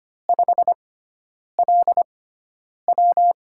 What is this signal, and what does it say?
Morse code, heard as a keyed tone, about 25 wpm, spaced as for 9 wpm, 700 Hz. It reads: 5LW
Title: USA Call Signs - 5 Characters - 6x Character Spacing - 25wpm